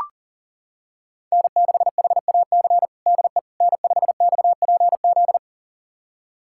Morse code, heard as a keyed tone, about 37 wpm, 700 Hz. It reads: N6HAC DE N5XPZ
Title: USA Callsign DE Callsign - 5 Characters - 40wpm